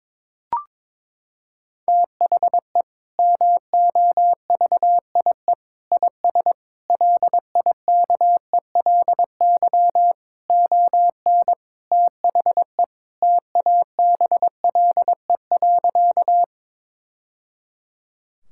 Morse code, heard as a keyed tone, about 22 wpm, 700 Hz. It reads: THE MOVIE IS LIKELY ON THE TABLE.
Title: Sentences from Top 1000 Words - Review Part 5 - 22wpm